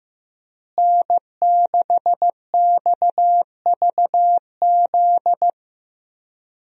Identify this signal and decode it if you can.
Morse code, heard as a keyed tone, about 15 wpm, 700 Hz. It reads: N6XVZ
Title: USA Call Signs - 5 Characters - Repeated 3x - 15wpm